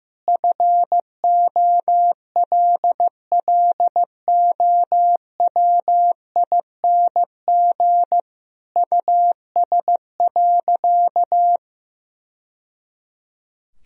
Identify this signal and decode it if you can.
Morse code, heard as a keyed tone, about 16 wpm, 700 Hz. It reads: FOLLOWING US.
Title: Sentences from Top 1000 Words - Review Part 4 - 15wpm